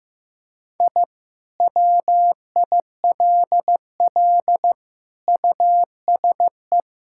Morse code, heard as a keyed tone, about 15 wpm, 700 Hz. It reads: I WILL USE
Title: Sentences from Top 1000 Words - Review Part 6 - 15wpm